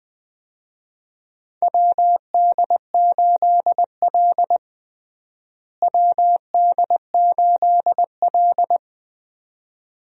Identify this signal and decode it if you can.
Morse code, heard as a keyed tone, about 20 wpm, 700 Hz. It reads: WD8L WD8L
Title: USA Call Signs - 4 Characters - Repeated 2x - 20wpm